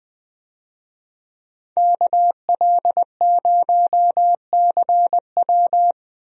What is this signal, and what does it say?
Morse code, heard as a keyed tone, about 20 wpm, 700 Hz. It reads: KL0CW
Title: USA Call Signs - 5 Characters - Repeated 3x - 20wpm